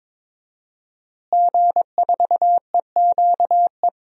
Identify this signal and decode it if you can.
Morse code, heard as a keyed tone, about 21 wpm, 700 Hz. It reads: G4EQE
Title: DXCC-100 Call Signs - 5 Characters - Repeated 2x - 22wpm